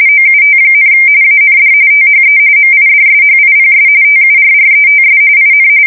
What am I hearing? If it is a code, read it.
RTTY, about 45.45 baud. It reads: OUT, AND THE MIDSHIPS TURRETS IN
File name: fsk170 / battleships